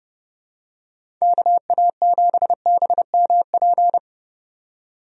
Morse code, heard as a keyed tone, about 31 wpm, 700 Hz. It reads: KA7BMP
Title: USA Call Signs - 6 Characters - 30wpm